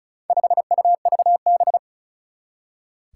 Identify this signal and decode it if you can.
Morse code, heard as a keyed tone, about 36 wpm, 700 Hz. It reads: 5UVB
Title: Four Challenging Letter-Number Combinations 35wpm